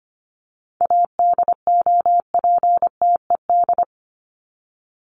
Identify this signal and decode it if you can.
Morse code, heard as a keyed tone, about 25 wpm, 700 Hz. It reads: ADOPTED
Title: ED Suffix - Top 100 Words with Base Words 25wpm